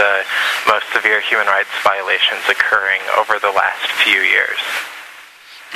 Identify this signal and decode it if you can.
Emergency Alert System (SAME) tones, heard as harsh buzzing bursts, with speech. It is heard from 0.8 to 4.6 s.